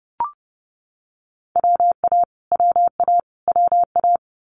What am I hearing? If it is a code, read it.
Morse code, heard as a keyed tone, about 30 wpm, 700 Hz. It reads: WA WA WA
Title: Multi-letter Phonemes Repeated 3x 30wpm